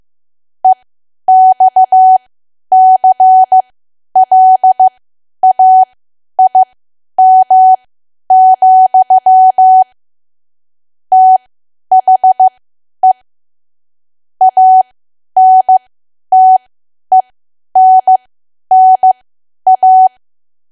Morse code, heard as a keyed tone, about 15 wpm, spaced as for 10 wpm, 750 Hz. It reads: EXCLAIM, THE ANTENNA
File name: Archive / 10 WPM / 10WPM